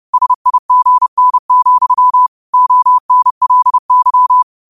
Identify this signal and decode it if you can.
Morse code, heard as a keyed tone, about 30 wpm, 1000 Hz. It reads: SIGN, ONLY
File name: qotdfiles / 30wpm